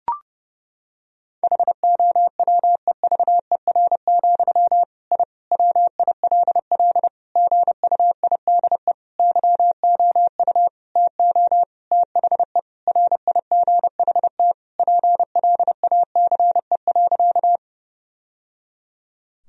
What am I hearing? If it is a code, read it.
Morse code, heard as a keyed tone, about 30 wpm, 700 Hz. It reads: HOWEVER, I WILL GUIDE YOU TO THE RIGHT PLACE.
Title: Sentences from Top 1000 Words - Review Part 3 - 30wpm